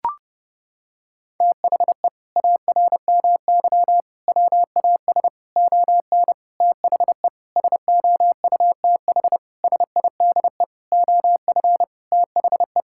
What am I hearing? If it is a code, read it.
Morse code, heard as a keyed tone, about 30 wpm, 700 Hz. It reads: THE ARMY WAS ON THE SOUTH SIDE OF THE
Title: Sentences from Top 1000 Words - Review Part 5 - 30wpm